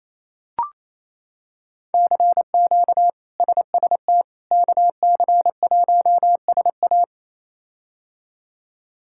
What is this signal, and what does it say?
Morse code, heard as a keyed tone, about 28 wpm, 700 Hz. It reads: CQ SST KC1SA
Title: K1USN SST QSO Practice - 28wpm